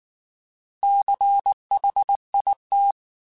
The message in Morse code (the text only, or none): CHIT